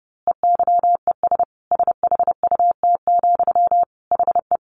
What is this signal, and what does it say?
Morse code, heard as a keyed tone, about 30 wpm, 700 Hz. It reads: EYES SHUT, HE
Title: A Sound Of Thunder 30wpm